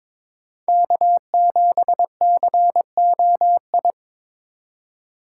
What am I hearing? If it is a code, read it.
Morse code, heard as a keyed tone, about 22 wpm, 700 Hz. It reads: K7COI
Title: USA Call Signs - 5 Characters - 3x Character Spacing - 22wpm